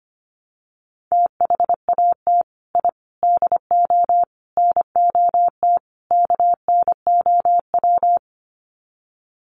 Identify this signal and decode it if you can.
Morse code, heard as a keyed tone, about 25 wpm, 700 Hz. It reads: THAT I DO NOT KNOW
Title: Sets of 5 Words Encore2 25wpm